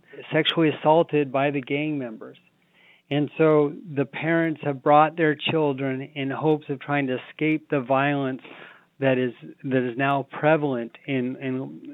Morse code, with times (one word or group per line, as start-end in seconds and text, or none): none